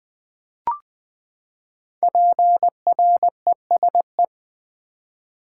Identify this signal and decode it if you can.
Morse code, heard as a keyed tone, about 20 wpm, 700 Hz. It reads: PRESE
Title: Build A Word - 5 Letters At A Time - 12-Letter Words - 20wpm